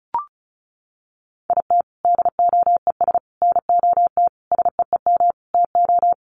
Morse code, heard as a keyed tone, about 34 wpm, 700 Hz. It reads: IT DOES NOT SEEM TO
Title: Sets of 5 Words Encore2 35wpm